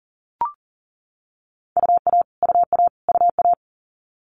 Morse code, heard as a keyed tone, about 38 wpm, 690 Hz. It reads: UA UA UA
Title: Multi-letter Phonemes Repeated 3x 40wpm